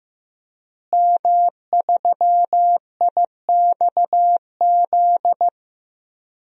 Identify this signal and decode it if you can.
Morse code, heard as a keyed tone, about 15 wpm, 700 Hz. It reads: M3IXZ
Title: DXCC-100 Call Signs - 5 Characters - Repeated 3x - 15wpm